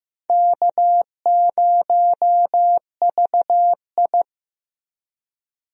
Morse code, heard as a keyed tone, about 15 wpm, 700 Hz. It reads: K0VI
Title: USA Call Signs - 4 Characters - Repeated 3x - 15wpm